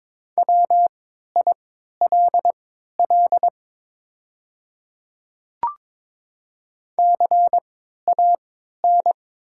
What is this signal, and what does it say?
Morse code, heard as a keyed tone, about 22 wpm, spaced as for 12 wpm, 700 Hz. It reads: WILL CAN
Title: 3x Character Spacing - Top 300 Words - 22wpm